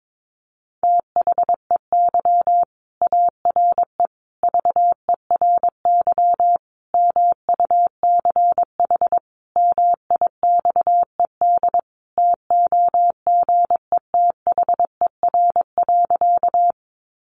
Morse code, heard as a keyed tone, about 22 wpm, 700 Hz. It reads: THEY ARE VERY MUCH MIXED TOGETHER.
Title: Sentences from Top 1500 Words 22wpm